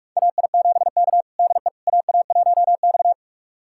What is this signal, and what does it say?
Morse code, heard as a keyed tone, about 46 wpm, 700 Hz. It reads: AI7K DE AA1X
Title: USA Callsign DE Callsign - 4 Characters - 45wpm